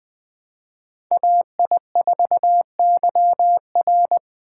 Morse code, heard as a keyed tone, about 21 wpm, 700 Hz. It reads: AI4YR